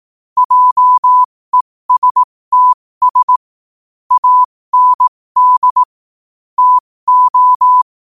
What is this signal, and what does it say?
Morse code, heard as a keyed tone, about 18 wpm, spaced as for 15 wpm, 1000 Hz. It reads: JESTS AND TO
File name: qotdfiles / 15wpm